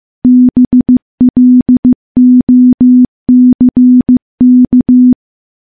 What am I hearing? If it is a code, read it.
Morse code, heard as a keyed tone, about 15 wpm, 250 Hz. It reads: BLOCK
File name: morse / morse-training-15WPM